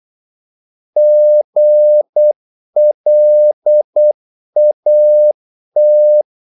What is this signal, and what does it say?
Morse code, heard as a keyed tone, about 8 wpm, 600 Hz. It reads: GLAT
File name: Korte ord DK 8wpm